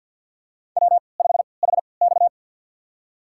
Morse code, heard as a keyed tone, about 49 wpm, spaced as for 28 wpm, 700 Hz. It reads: W5HX
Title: USA Call Signs - 4 Characters - 3x Character Spacing - 50wpm